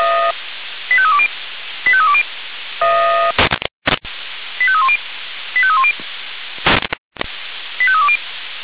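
Five-tone selective call (ZVEI), heard as quick runs of five tones, tones of about 70 ms. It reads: 85320 85320 85310 85310 85320
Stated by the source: ZVEI five-tone selective call 85265 received by SDR.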